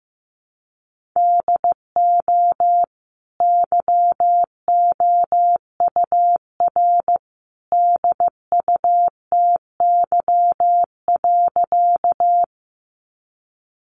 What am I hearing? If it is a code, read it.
Morse code, heard as a keyed tone, about 15 wpm, 700 Hz. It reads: DO YOUR DUTY.
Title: Sentences from Top 1500 Words 15wpm